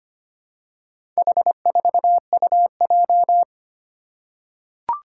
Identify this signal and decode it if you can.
Morse code, heard as a keyed tone, about 25 wpm, 700 Hz. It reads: H4UJ
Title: Four Challenging Letter-Number Combinations 25wpm